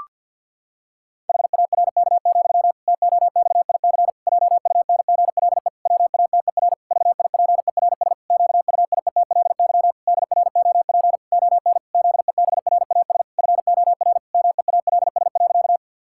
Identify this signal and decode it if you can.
Morse code, heard as a keyed tone, about 50 wpm, 700 Hz. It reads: HARK, TOXIC JUNGLE WATER VIPERS QUIETLY DROP ON ZEBRAS FOR MEALS.
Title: Pangram Challenge v2 - 50wpm